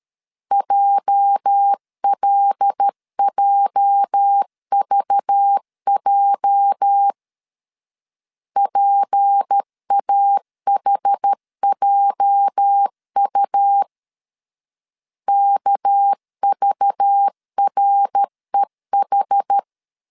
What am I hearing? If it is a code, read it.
Morse code, heard as a keyed tone, about 13 wpm, 800 Hz. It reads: JLJVJ PAHJU KVREH
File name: LEZIONI / 8 YJ45 / 8 YJ 12wpm 2235